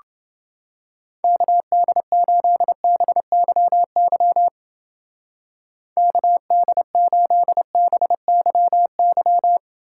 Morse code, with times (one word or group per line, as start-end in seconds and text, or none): 1.2-4.5 KD8BYY
6.0-9.6 KD8BYY